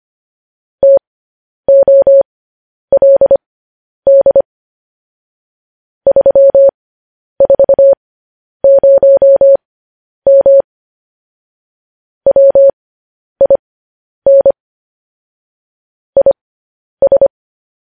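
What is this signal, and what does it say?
Morse code, heard as a keyed tone, about 25 wpm, spaced as for 10 wpm, 550 Hz. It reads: TOLD 340M WIN IS
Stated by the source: Morse code news headlines.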